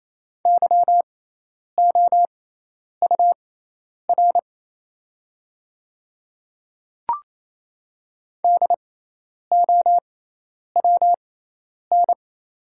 Morse code, heard as a keyed tone, about 28 wpm, spaced as for 10 wpm, 700 Hz. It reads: YOUR DOWN
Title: 6x Character Spacing - Top 100 Words - 28wpm